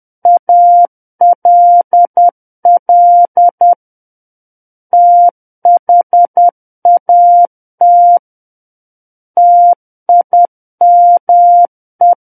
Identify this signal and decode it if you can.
Morse code, heard as a keyed tone, about 10 wpm, 700 Hz. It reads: ALL THAT TIME